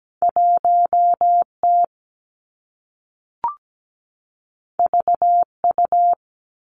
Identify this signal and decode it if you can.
Morse code, heard as a keyed tone, about 17 wpm, 700 Hz. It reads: 1T VU
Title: Two Challenging Letter-Number Combinations 17wpm